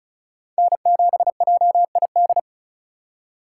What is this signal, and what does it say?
Morse code, heard as a keyed tone, about 34 wpm, 700 Hz. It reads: N7JID
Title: USA Call Signs - 5 Characters - 3x Character Spacing - 35wpm